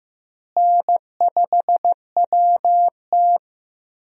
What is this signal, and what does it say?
Morse code, heard as a keyed tone, about 15 wpm, 700 Hz. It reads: N5WT
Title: USA Call Signs - 4 Characters - Repeated 2x - 15wpm